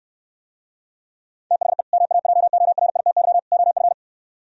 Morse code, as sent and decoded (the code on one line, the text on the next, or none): - .... / -- .- .--- --- .-. .. - -.-- / --- ..-.
THE MAJORITY OF